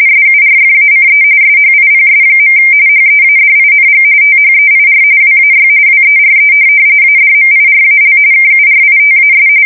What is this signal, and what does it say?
RTTY, about 45.45 baud. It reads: SURFACE SHIPS. HEADING THEIR WAY WERE ADMIRAL KURITAS
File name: fsk170 / USS-New-Jersey